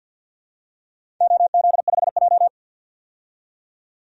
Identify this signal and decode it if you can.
Morse code, heard as a keyed tone, about 51 wpm, 700 Hz. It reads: OZ5J
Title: DXCC-100 Call Signs - 4 Characters - 50wpm